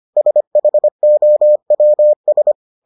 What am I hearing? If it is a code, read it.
Morse code, heard as a keyed tone, about 25 wpm, 600 Hz. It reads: SHOWS